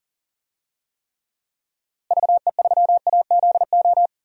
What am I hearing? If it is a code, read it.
Morse code, heard as a keyed tone, about 39 wpm, 700 Hz. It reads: VE3AZO
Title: DXCC-100 Call Signs - 6 Characters - Repeated 2x - 40wpm